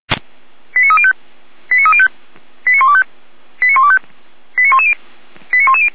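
Five-tone selective call (ZVEI), heard as quick runs of five tones, tones of about 70 ms. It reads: 89305 89305 89125 89125 89110 89110
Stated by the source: ZVEI five-tone selective call received on an SDR.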